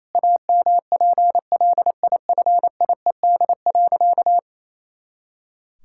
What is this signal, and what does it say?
Morse code, heard as a keyed tone, about 28 wpm, 700 Hz. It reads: AMPLIFIED.
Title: Pangram Challenge v2 - 28wpm